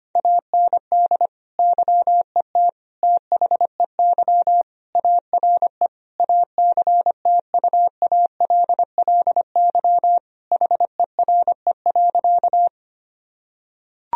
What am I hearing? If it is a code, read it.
Morse code, heard as a keyed tone, about 25 wpm, 700 Hz. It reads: AND YET THEY ARE ACTUALLY HERE.
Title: Sentences from Top 1000 Words - Review Part 4 - 25wpm